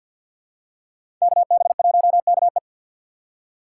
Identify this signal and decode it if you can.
Morse code, heard as a keyed tone, about 47 wpm, 700 Hz. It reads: KB1KE
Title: USA Call Signs - 5 Characters - 2x Character Spacing - 50wpm